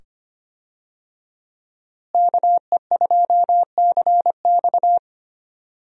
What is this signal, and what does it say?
Morse code, heard as a keyed tone, about 25 wpm, 700 Hz. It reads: KE2CX
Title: USA Call Signs - 5 Characters - Repeated 2x - 25wpm